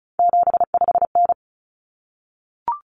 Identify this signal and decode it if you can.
Morse code, heard as a keyed tone, about 34 wpm, 700 Hz. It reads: 75N
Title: Common Cut Numbers - 3-Digit 35wpm